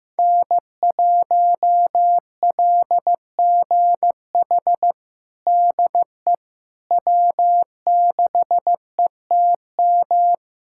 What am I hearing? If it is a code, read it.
Morse code, heard as a keyed tone, about 15 wpm, 700 Hz. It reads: N1LGH DE W6ETM
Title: USA Callsign DE Callsign - 5 Characters - 15wpm